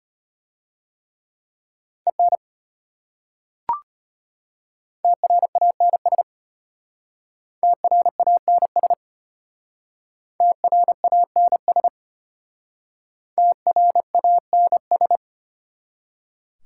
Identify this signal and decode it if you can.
Morse code, heard as a keyed tone, about 31 wpm, 700 Hz. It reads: EN TRANS TRANS TRANS TRANS
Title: Speed-Racing Prefixes and Suffixes 25wpm